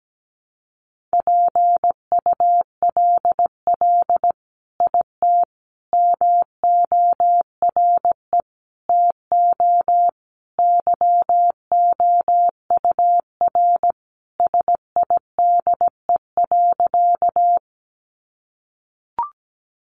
Morse code, heard as a keyed tone, about 17 wpm, 700 Hz. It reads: PULL IT MORE TO YOUR SIDE.
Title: Sentences from Top 400 Words 17wpm